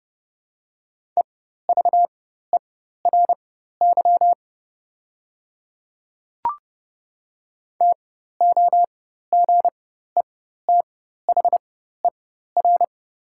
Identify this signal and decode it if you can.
Morse code, heard as a keyed tone, about 30 wpm, spaced as for 14 wpm, 700 Hz. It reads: EVERY TOGETHER
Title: 4x Character Spacing - Top 300 Words - 30wpm